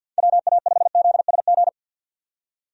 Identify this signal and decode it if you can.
Morse code, heard as a keyed tone, about 49 wpm, 700 Hz. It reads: WA5ZSG
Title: USA Call Signs - 6 Characters - 50wpm